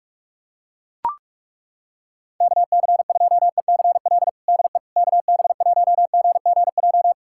Morse code, heard as a keyed tone, about 42 wpm, 700 Hz. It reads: KC2ECL DE KB1GGJ
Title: USA Callsign DE Callsign - 6 Characters - 45wpm